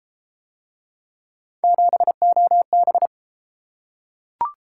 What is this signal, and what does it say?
Morse code, heard as a keyed tone, about 32 wpm, 700 Hz. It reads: 7OB